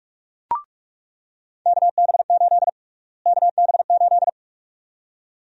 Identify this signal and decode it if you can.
Morse code, heard as a keyed tone, about 45 wpm, 700 Hz. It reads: KB8 KB8
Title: USA Call Sign Prefixes - Repeated 2x - 45wpm